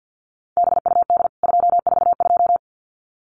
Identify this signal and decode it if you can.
Morse code, heard as a keyed tone, about 50 wpm, 700 Hz. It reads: BUD 242
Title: CWT Exchange 50wpm